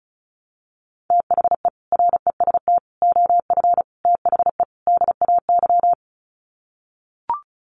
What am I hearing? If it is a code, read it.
Morse code, heard as a keyed tone, about 34 wpm, 700 Hz. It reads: THE REST OF THE DAY
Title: Sets of 5 Words 35wpm